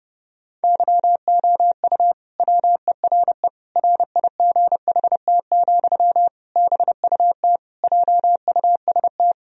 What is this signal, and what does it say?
Morse code, heard as a keyed tone, about 30 wpm, 700 Hz. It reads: YOU WERE RIGHT, BUT JUST
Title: Sentences from Top 1000 Words - Review Part 2 - 30wpm